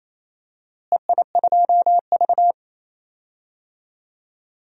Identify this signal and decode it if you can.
Morse code, heard as a keyed tone, about 28 wpm, 700 Hz. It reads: EI2V
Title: DXCC-100 Call Signs - 4 Characters - Repeated 3x - 28wpm